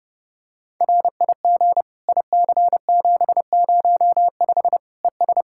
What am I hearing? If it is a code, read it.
Morse code, heard as a keyed tone, about 30 wpm, 700 Hz. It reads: RIG IC705 ES
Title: Practice QSO - Exchange 2 - Info - 30wpm